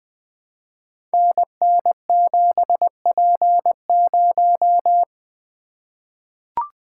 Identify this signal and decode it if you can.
Morse code, heard as a keyed tone, about 20 wpm, 700 Hz. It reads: NN7P0
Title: Code Groups - 5 Letter:Numbers - 20wpm